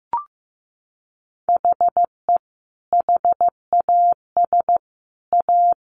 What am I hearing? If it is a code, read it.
Morse code, heard as a keyed tone, about 15 wpm, 700 Hz. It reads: HE HAS A
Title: Sentences from Top 500 Words 15wpm